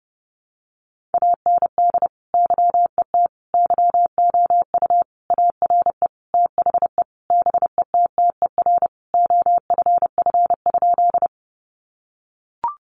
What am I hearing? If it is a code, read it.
Morse code, heard as a keyed tone, about 30 wpm, 700 Hz. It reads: AND YET YOU ARE THE BETTER OFF?
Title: Sentences from Top 500 Words 30wpm